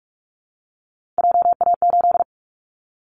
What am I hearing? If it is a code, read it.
Morse code, heard as a keyed tone, about 44 wpm, 700 Hz. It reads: JA8
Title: Callsign Prefixes for Top 50 DXCC - 45wpm